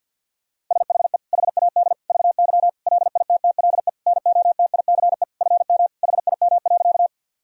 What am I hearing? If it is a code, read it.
Morse code, heard as a keyed tone, about 50 wpm, 700 Hz. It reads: SHE HAD VY LITTLE NOTICE FM HIM.
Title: Sentences from Top 400 Words and Abbreviations 50wpm